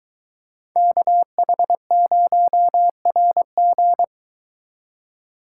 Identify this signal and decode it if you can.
Morse code, heard as a keyed tone, about 23 wpm, 700 Hz. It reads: KH0RG